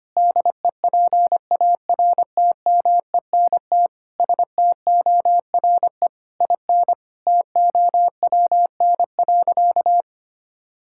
Morse code, heard as a keyed tone, about 25 wpm, 700 Hz. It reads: DEPARTMENT STORE IN TOWN.
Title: Sentences from Top 1000 Words - Review Part 5 - 25wpm